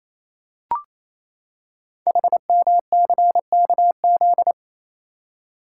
Morse code, heard as a keyed tone, about 27 wpm, 700 Hz. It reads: HMCKZ
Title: Code Groups - 5 Letters - 28wpm